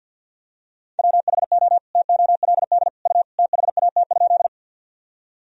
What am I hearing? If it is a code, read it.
Morse code, heard as a keyed tone, about 52 wpm, 700 Hz. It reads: WHO TOLD U THAT?
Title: Sentences from Top 400 Words and Abbreviations 50wpm